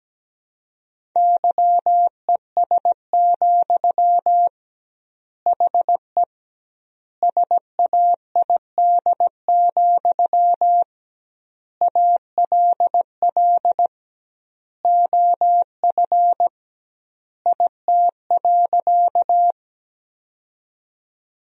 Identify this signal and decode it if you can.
Morse code, heard as a keyed tone, about 17 wpm, 700 Hz. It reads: YES, HE SAID, ALL OF IT.
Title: 2x Word-Spacing - Sentences from Top 500 Words - 17wpm